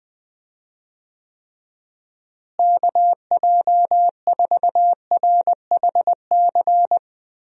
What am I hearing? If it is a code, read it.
Morse code, heard as a keyed tone, about 20 wpm, 700 Hz. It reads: KJ4RHC
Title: USA Call Signs - 6 Characters - Repeated 2x - 20wpm